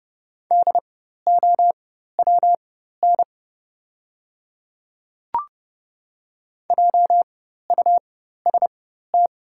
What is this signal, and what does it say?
Morse code, heard as a keyed tone, about 30 wpm, spaced as for 14 wpm, 700 Hz. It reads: DOWN JUST
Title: 4x Character Spacing - Top 200 Words - 30wpm